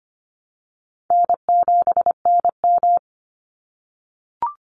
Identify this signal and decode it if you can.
Morse code, heard as a keyed tone, about 25 wpm, 700 Hz. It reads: N7NM
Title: Call Signs - 25wpm